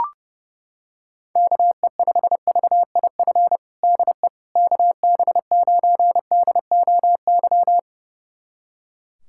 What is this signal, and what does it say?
Morse code, heard as a keyed tone, about 30 wpm, 700 Hz. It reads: KE5VIF DE KB9DOY
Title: USA Callsign DE Callsign - 6 Characters - 30wpm